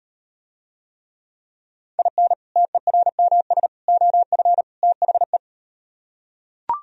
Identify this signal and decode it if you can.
Morse code, heard as a keyed tone, about 39 wpm, 700 Hz. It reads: IN TERMS OF THE